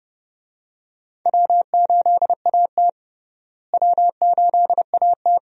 Morse code, heard as a keyed tone, about 30 wpm, 700 Hz. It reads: W8AT W8AT
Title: USA Call Signs - 4 Characters - Repeated 2x - 30wpm